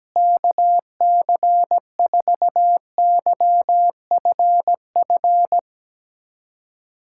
Morse code, heard as a keyed tone, about 17 wpm, 700 Hz. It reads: KC4YFF
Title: USA Callsign DE Callsign - 6 Characters - 17wpm